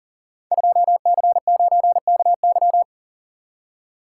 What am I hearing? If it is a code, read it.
Morse code, heard as a keyed tone, about 40 wpm, 700 Hz. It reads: 2C9KY
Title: Code Groups - 5 Letter:Numbers - 40wpm